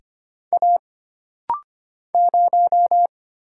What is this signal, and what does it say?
Morse code, heard as a keyed tone, about 25 wpm, 700 Hz. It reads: A 0
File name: Single Character (Letter:Number:Punctuation:Prosign) - Rapid-Fire - 25wpm